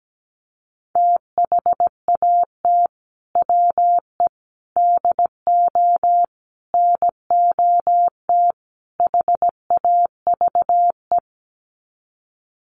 Morse code, heard as a keyed tone, about 17 wpm, 700 Hz. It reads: THAT WE DO NOT HAVE
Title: Sets of 5 Words Encore 17wpm